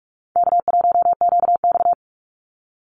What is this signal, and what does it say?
Morse code, heard as a keyed tone, about 44 wpm, 700 Hz. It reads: K1QX
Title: Easy Call Signs - 45wpm